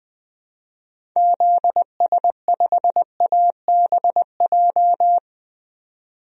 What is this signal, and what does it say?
Morse code, heard as a keyed tone, about 20 wpm, 700 Hz. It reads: ZS5ABJ
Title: DXCC-100 Call Signs - 6 Characters - 20wpm